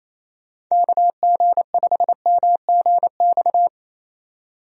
Morse code, heard as a keyed tone, about 28 wpm, 700 Hz. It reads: KG5MGX
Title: USA Call Signs - 6 Characters - 28wpm